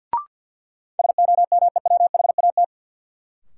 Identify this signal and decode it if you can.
Morse code, heard as a keyed tone, about 52 wpm, 700 Hz. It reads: SOMEWHAT
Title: Sentences from Top 1000 Words - Review Part 1 - New Word Drill - 50wpm